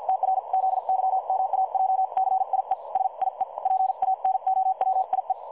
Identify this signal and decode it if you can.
Morse code, heard as a keyed tone, about 51 wpm, spaced as for 43 wpm, 760 Hz. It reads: TRZYMAJCIE SIE PANOWIE
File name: dlazaawansowanych / nagraniazpasma / websdr recording 3565.0kHz